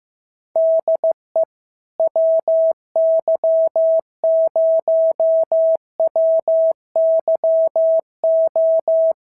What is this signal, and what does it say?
Morse code, heard as a keyed tone, about 14 wpm, 650 Hz. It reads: DE WY0WYO
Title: SOTA QSO Practice 15wpm